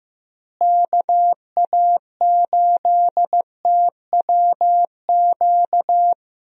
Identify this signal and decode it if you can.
Morse code, heard as a keyed tone, about 15 wpm, 700 Hz. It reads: KA8TWQ
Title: USA Call Signs - 6 Characters - Repeated 3x - 15wpm